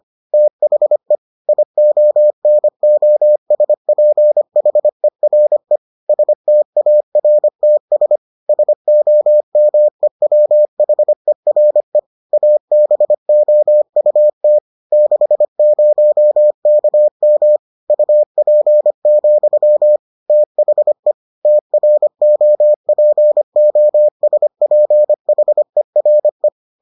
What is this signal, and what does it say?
Morse code, heard as a keyed tone, about 25 wpm, 600 Hz. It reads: THE IONOSPHERE STARTS SOMEWHERE ABOUT 60KM UP, THE TROPOSPHERE